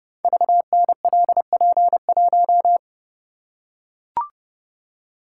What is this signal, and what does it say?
Morse code, heard as a keyed tone, about 29 wpm, 690 Hz. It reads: VNLP1
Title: Code Groups - 5 Letter:Numbers - 30wpm